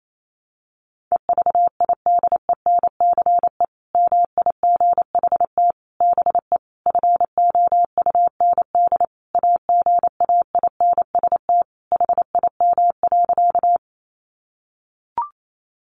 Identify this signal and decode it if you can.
Morse code, heard as a keyed tone, about 28 wpm, 700 Hz. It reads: EVIDENCE MIGHT BE FOUND AGAINST HIM.
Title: Sentences from Top 1300 Words 28wpm